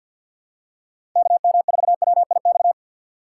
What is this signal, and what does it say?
Morse code, heard as a keyed tone, about 53 wpm, 700 Hz. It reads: KM4WIX
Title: USA Call Signs - 6 Characters - 6x Character Spacing - 50wpm